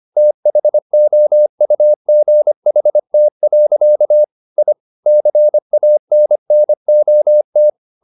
Morse code, heard as a keyed tone, about 25 wpm, 600 Hz. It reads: THOUGHT. I CANNOT